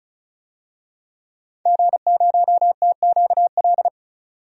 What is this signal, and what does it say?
Morse code, heard as a keyed tone, about 36 wpm, 700 Hz. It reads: G0TQL
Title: DXCC-100 Call Signs - 5 Characters - 35wpm